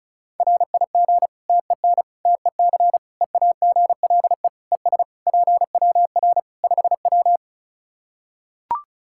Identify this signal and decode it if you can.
Morse code, heard as a keyed tone, about 36 wpm, 700 Hz. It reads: RIG TEN TEC EAGLE ES PWR 5W
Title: Practice QSO - Exchange 2 - Info - 35wpm